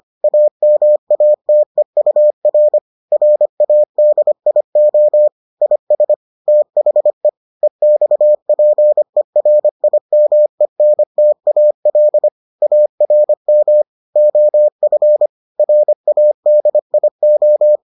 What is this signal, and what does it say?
Morse code, heard as a keyed tone, about 25 wpm, 600 Hz. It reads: AMATEUR RADIO IS THE EXPERIMENTAL ARM OF RADIO